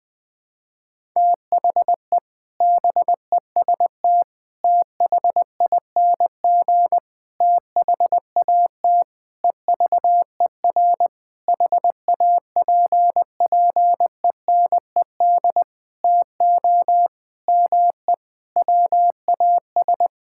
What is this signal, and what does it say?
Morse code, heard as a keyed tone, about 20 wpm, 700 Hz. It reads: THE BEST THING THAT EVER HAPPENED TO ME WAS
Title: Sentences with 7-Word Phrases - Spoken Beginning - 20wpm